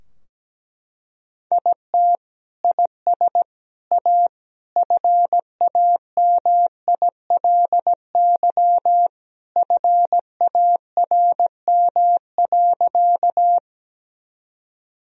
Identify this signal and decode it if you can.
Morse code, heard as a keyed tone, about 17 wpm, 700 Hz. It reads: IT IS A FAMILY FARM.
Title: Sentences from Top 1000 Words - Review Part 2 - 17wpm